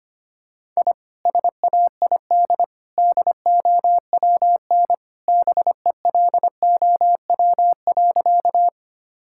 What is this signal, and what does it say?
Morse code, heard as a keyed tone, about 25 wpm, 700 Hz. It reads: I SAID DOWN BELOW.
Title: Sentences from Top 1000 Words - Review Part 5 - 25wpm